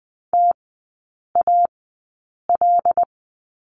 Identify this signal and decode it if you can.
Morse code, heard as a keyed tone, about 20 wpm, spaced as for 8 wpm, 700 Hz. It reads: TAL
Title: 100 Lesson 17e - Letter u - Three Characters 20wpm